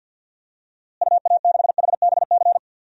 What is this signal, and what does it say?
Morse code, heard as a keyed tone, about 51 wpm, 700 Hz. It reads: UA6HBC
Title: DXCC-100 Call Signs - 6 Characters - Repeated 2x - 50wpm